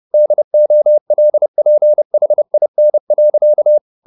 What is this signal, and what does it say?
Morse code, heard as a keyed tone, about 30 wpm, 600 Hz. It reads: DOLPHIN.